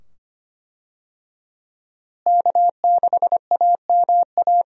Morse code, heard as a keyed tone, about 25 wpm, 700 Hz. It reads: K6AMA
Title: USA Call Signs - 5 Characters - Repeated 2x - 25wpm